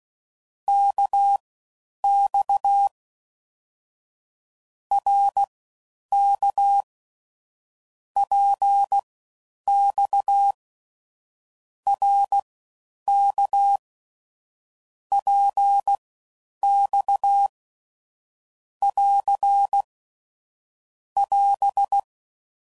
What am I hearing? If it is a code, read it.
Morse code, heard as a keyed tone, about 16 wpm, spaced as for 9 wpm, 800 Hz. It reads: KX RK PX RK PX <AR> <AS>